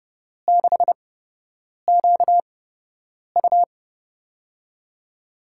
Morse code, heard as a keyed tone, about 30 wpm, spaced as for 8 wpm, 700 Hz. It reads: 6QU
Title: USA Call Signs - 8x Character Spacing - 30wpm